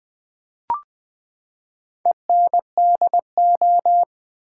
Morse code, heard as a keyed tone, about 20 wpm, 700 Hz. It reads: ENDO
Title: Build A Word - 4 Letters At A Time - 11-Letter Words - 20wpm